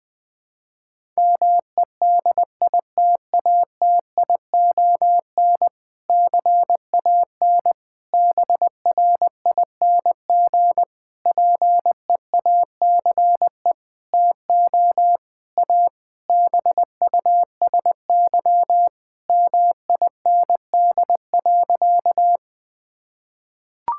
Morse code, heard as a keyed tone, about 19 wpm, 700 Hz. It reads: MEDITATION CAN BRING PEACE TO A BUSY MIND.
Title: Difficult Sentences with Hints - 20wpm